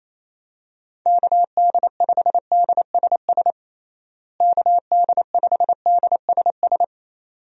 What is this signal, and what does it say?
Morse code, heard as a keyed tone, about 28 wpm, 700 Hz. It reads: KD5DSS KD5DSS
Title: USA Call Signs - 6 Characters - Repeated 2x - 28wpm